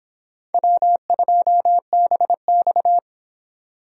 Morse code, heard as a keyed tone, about 23 wpm, 700 Hz. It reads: W2BX